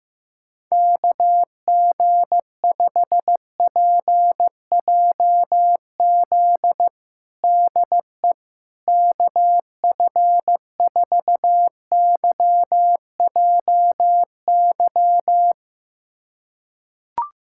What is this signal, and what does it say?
Morse code, heard as a keyed tone, about 15 wpm, 700 Hz. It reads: KG5PJZ DE KF4YJY
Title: USA Callsign DE Callsign - 6 Characters - 15wpm